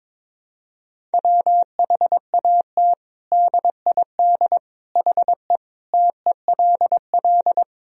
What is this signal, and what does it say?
Morse code, heard as a keyed tone, about 22 wpm, 700 Hz. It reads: WHAT DID HE TELL
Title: Sentences from Top 200 Words and Abbreviations 22wpm